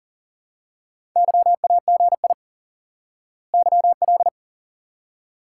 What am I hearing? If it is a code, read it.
Morse code, heard as a keyed tone, about 40 wpm, 700 Hz. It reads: YAGI YL
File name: IQE Course 3 - Lesson 89 - YAGI - Two Elements - 40wpm